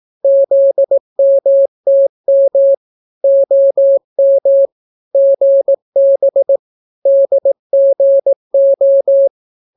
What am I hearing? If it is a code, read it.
Morse code, heard as a keyed tone, about 18 wpm, 550 Hz. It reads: ZMTM OM GB DGO